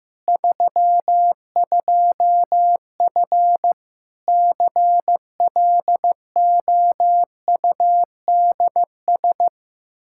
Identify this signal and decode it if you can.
Morse code, heard as a keyed tone, about 15 wpm, 700 Hz. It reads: 32F CLOUDS
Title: Practice QSO - Exchange 2 - Info - 15wpm